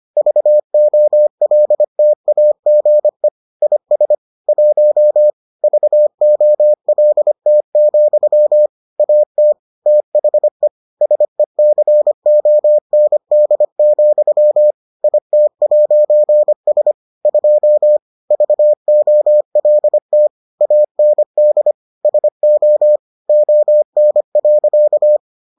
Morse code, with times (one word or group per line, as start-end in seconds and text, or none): 0.2-3.3 VOLTAGE
3.6-4.1 IS
4.5-5.3 1
5.6-8.7 VOLT,
9.0-9.5 AT
9.9-10.7 THE
11.0-14.7 SECOND,
15.0-16.9 IT'S
17.3-18.0 2
18.3-20.3 VOLT
20.6-21.7 AND
22.1-23.0 SO
23.3-25.2 ON.